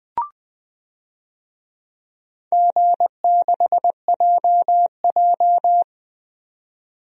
Morse code, heard as a keyed tone, about 20 wpm, 700 Hz. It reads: G6JJ
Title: DXCC-100 Call Signs - 4 Characters - 20wpm